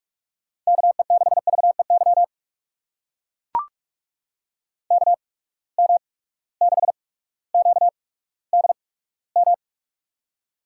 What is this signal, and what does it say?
Morse code, heard as a keyed tone, about 42 wpm, spaced as for 19 wpm, 700 Hz. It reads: KE6VEY KM6QDM
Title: USA Call Signs - 6 Characters - 8x Character Spacing - 45wpm